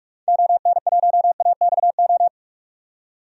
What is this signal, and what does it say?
Morse code, heard as a keyed tone, about 45 wpm, 700 Hz. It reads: ON1AXO
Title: DXCC-100 Call Signs - 6 Characters - 45wpm